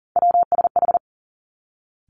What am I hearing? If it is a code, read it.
Morse code, heard as a keyed tone, about 41 wpm, 700 Hz. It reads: WSH
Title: Three Challenging Letter-Number Combinations 40wpm